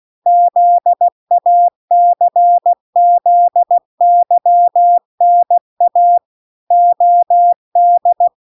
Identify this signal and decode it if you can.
Morse code, heard as a keyed tone, about 16 wpm, 700 Hz. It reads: ZACZYNA OD